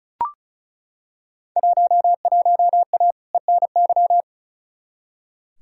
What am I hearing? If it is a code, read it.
Morse code, heard as a keyed tone, about 36 wpm, 700 Hz. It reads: 11A ENY
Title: Field Day Exchanges Encore - 35wpm